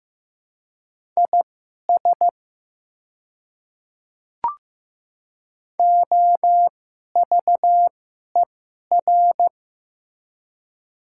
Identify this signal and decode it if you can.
Morse code, heard as a keyed tone, about 15 wpm, spaced as for 11 wpm, 700 Hz. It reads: IS OVER
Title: 2x Character Spacing - Top 100 Words - 15wpm